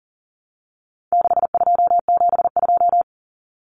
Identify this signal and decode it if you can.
Morse code, heard as a keyed tone, about 40 wpm, 700 Hz. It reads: 6272
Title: Numbers - 4-Digit 40wpm